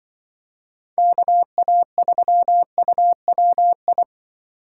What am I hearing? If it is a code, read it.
Morse code, heard as a keyed tone, about 24 wpm, 700 Hz. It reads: KA3UWI
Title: Speed Racing - USA 2x3 Call Signs 20wpm